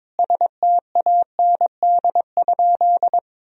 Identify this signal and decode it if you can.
Morse code, heard as a keyed tone, about 22 wpm, 700 Hz. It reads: STAND?
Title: Sentences from Top 1000 Words - Review Part 3 - 22wpm